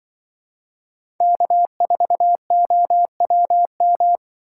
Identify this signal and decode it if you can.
Morse code, heard as a keyed tone, about 24 wpm, 700 Hz. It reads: K4OWM
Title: Speed Racing - USA 1x3 Call Signs 20wpm